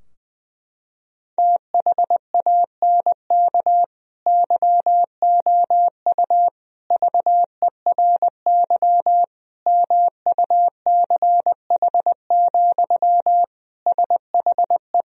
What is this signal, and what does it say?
Morse code, heard as a keyed tone, about 20 wpm, 700 Hz. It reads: THANK YOU VERY MUCH, SHE
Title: Sentences from Top 1000 Words - Review Part 4 - 20wpm